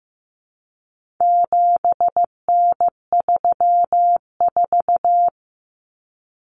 Morse code, heard as a keyed tone, about 15 wpm, 700 Hz. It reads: 7N34
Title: Common Cut Numbers - 4-Digit 15wpm